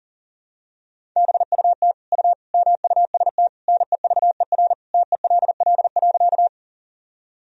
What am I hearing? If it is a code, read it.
Morse code, heard as a keyed tone, about 40 wpm, 700 Hz. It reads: BUT U MUST NEVER TELL.
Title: Sentences from Top 300 Words and Abbreviations 40wpm